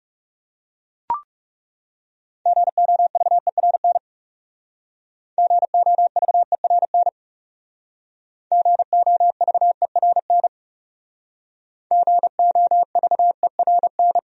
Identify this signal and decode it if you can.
Morse code, heard as a keyed tone, about 38 wpm, 700 Hz. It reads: GOVERN GOVERN GOVERN GOVERN